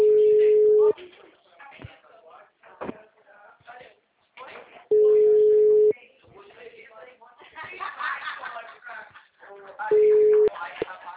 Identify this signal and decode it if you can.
Ringback tone (Europe, 425 Hz), with speech.